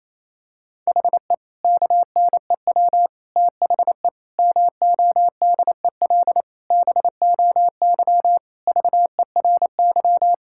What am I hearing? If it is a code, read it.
Morse code, heard as a keyed tone, about 28 wpm, 700 Hz. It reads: HE KNEW THE MODEL BOY VERY
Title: Sentences from Top 1000 Words - Review Part 5 - 28wpm